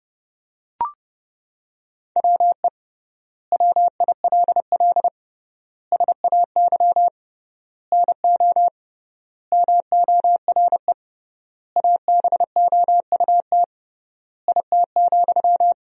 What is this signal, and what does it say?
Morse code, heard as a keyed tone, about 31 wpm, 700 Hz. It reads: WE WILL SAY NO MORE ABOUT IT,
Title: 3x Word-Spacing - Sentences from Top 200 Words - 30wpm